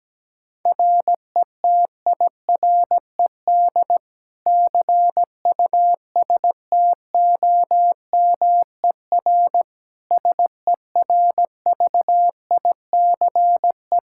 Morse code, heard as a keyed tone, about 17 wpm, 700 Hz. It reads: RETIRED CUSTOMER SERVICE
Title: Practice QSO - Exchange 3 - More Info - 17wpm